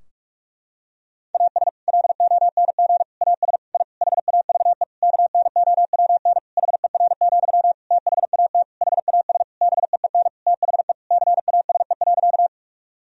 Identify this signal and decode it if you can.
Morse code, heard as a keyed tone, about 43 wpm, 700 Hz. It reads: AS LONG AS I HAVE KNOWN HER, THAT HAS BEEN THE CASE.
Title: Sentences from Top 1000 Words - Review Part 3 - 45wpm